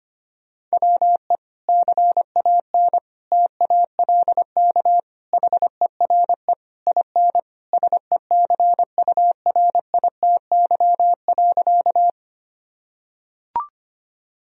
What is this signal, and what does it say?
Morse code, heard as a keyed tone, about 25 wpm, 700 Hz. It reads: WE CAN TALK HERE IN SECURITY.
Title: Sentences from Top 1000 Words - Review Part 4 - 25wpm